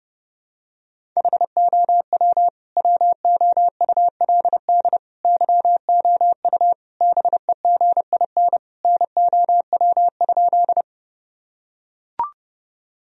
Morse code, heard as a keyed tone, about 30 wpm, 700 Hz. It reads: HOW WOULD YOU BEGIN NOW?
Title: Sentences from Top 1000 Words - Review Part 1 - 30wpm